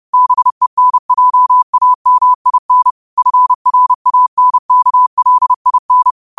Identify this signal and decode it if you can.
Morse code, heard as a keyed tone, about 30 wpm, 1000 Hz. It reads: BENJAMIN FRANKLIN
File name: qotdfiles / 30wpm